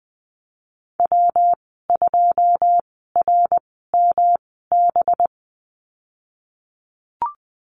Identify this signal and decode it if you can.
Morse code, heard as a keyed tone, about 20 wpm, spaced as for 15 wpm, 700 Hz. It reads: W2RMB
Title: 219 Lesson 35g - Number 0 - Call Signs 20wpm